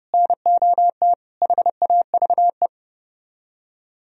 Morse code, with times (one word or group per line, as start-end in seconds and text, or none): 0.1-1.1 NOT
1.4-2.7 HAVE